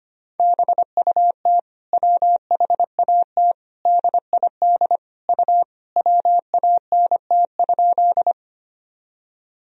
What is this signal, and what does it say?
Morse code, heard as a keyed tone, about 25 wpm, 700 Hz. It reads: BUT WHAT DID U WANT?
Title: Sentences from Top 200 Words and Abbreviations 25wpm